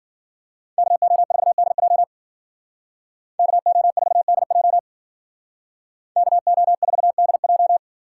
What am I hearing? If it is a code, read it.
Morse code, heard as a keyed tone, about 48 wpm, 700 Hz. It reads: KO4DJ KO4DJ KO4DJ
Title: Speed Racing - USA 2x2 Call Signs 40wpm